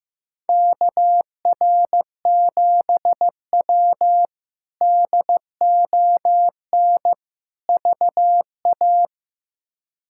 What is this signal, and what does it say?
Morse code, heard as a keyed tone, about 15 wpm, 700 Hz. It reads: KR7W DON VA
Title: K1USN SST QSO Practice - 15wpm